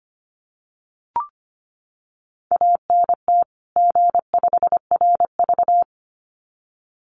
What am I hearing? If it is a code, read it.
Morse code, heard as a keyed tone, about 25 wpm, 700 Hz. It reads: ANT G5RV
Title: Common QSO Phrases 25wpm